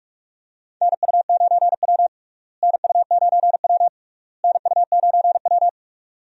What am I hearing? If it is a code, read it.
Morse code, heard as a keyed tone, about 46 wpm, 700 Hz. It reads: NU9W NU9W NU9W
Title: USA Call Signs - 4 Characters - Repeated 3x - 45wpm